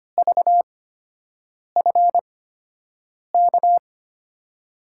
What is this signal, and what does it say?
Morse code, heard as a keyed tone, about 25 wpm, spaced as for 7 wpm, 700 Hz. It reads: VFK
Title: USA Call Signs - 6 Characters - 8x Character Spacing - 25wpm